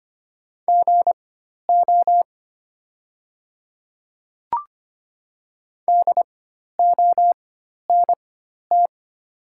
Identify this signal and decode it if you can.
Morse code, heard as a keyed tone, about 25 wpm, spaced as for 12 wpm, 700 Hz. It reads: GO DONT
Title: 4x Character Spacing - Top 500 Words - 25wpm